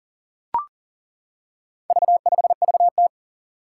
Morse code, heard as a keyed tone, about 40 wpm, 700 Hz. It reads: V5VT
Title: Four Challenging Letter-Number Combinations 40wpm